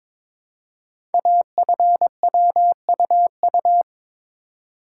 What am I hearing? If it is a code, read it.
Morse code, heard as a keyed tone, about 22 wpm, 700 Hz. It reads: AFWUU
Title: Code Groups - 5 Letters - 22wpm